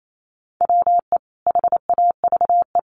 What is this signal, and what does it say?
Morse code, heard as a keyed tone, about 28 wpm, 700 Hz. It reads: WE HAVE
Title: Sets of 5 Words Encore 28wpm